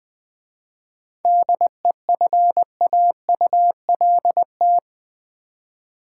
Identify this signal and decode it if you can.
Morse code, heard as a keyed tone, about 20 wpm, 700 Hz. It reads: DEFAULT
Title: 7 Letter Words - First 3 Letters Given - 20wpm